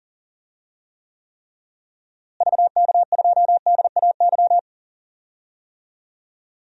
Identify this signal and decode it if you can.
Morse code, heard as a keyed tone, about 40 wpm, 700 Hz. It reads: VK2DAY
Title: DXCC-100 Call Signs - 6 Characters - Repeated 3x - 40wpm